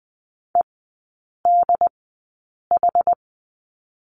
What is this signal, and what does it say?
Morse code, heard as a keyed tone, about 20 wpm, spaced as for 8 wpm, 700 Hz. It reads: EDH
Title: 067 Lesson 12e - Letter d - Three Characters 20wpm